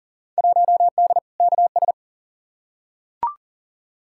Morse code, heard as a keyed tone, about 40 wpm, 700 Hz. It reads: 1D KS
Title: Field Day Exchanges - 40wpm